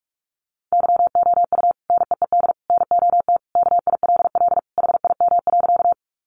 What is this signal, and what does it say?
Morse code, heard as a keyed tone, about 44 wpm, 700 Hz. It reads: YOU NEED NOT KILL HIM.
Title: Sentences from Top 1400 Words 45wpm